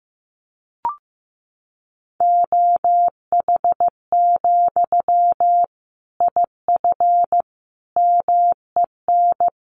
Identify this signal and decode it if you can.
Morse code, heard as a keyed tone, about 15 wpm, 700 Hz. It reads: OH, IF MEN
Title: Sentences from Top 500 Words 15wpm